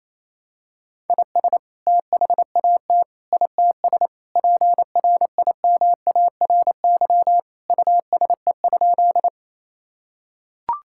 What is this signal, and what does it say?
Morse code, heard as a keyed tone, about 28 wpm, 700 Hz. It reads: IS THAT ITS PRIMARY USE?
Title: Sentences from Top 1000 Words - Review Part 5 - 28wpm